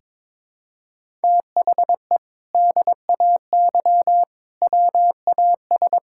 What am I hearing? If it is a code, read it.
Morse code, heard as a keyed tone, about 22 wpm, 700 Hz. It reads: THE DAY WAS
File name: practice / 22wpm / Sentences with 2-Word Phrases - 22wpm